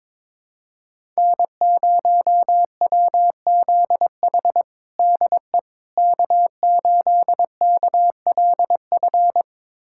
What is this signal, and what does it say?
Morse code, heard as a keyed tone, about 22 wpm, 700 Hz. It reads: N0WZH DE K8KLF
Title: USA Callsign DE Callsign - 5 Characters - 22wpm